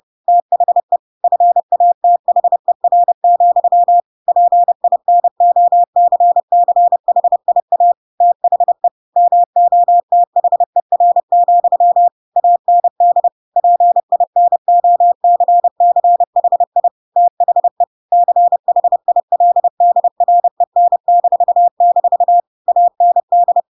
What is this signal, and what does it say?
Morse code, heard as a keyed tone, about 30 wpm, 700 Hz. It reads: THE FATHER, PINOCCHIA THE MOTHER, AND PINOCCHI THE CHILDREN-- AND
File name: Pinocchio / Pinocchio ch3 30 30 ik1hge